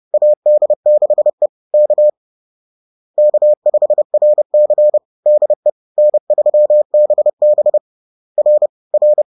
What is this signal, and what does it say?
Morse code, heard as a keyed tone, about 30 wpm, 600 Hz. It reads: AD6E K K5RC DE N3BB R R